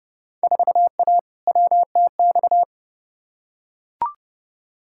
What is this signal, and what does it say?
Morse code, heard as a keyed tone, about 32 wpm, 700 Hz. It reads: 4A WTX
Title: Field Day Exchanges Encore - 30wpm